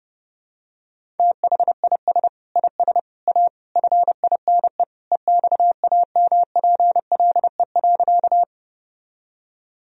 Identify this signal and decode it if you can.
Morse code, heard as a keyed tone, about 30 wpm, 700 Hz. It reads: THIS IS A FINE EXAMPLE.
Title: Sentences from Top 1000 Words - Review Part 1 - 30wpm